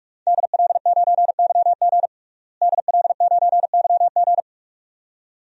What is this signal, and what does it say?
Morse code, heard as a keyed tone, about 46 wpm, 700 Hz. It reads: DL9YG DL9YG
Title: DXCC-100 Call Signs - 5 Characters - Repeated 2x - 45wpm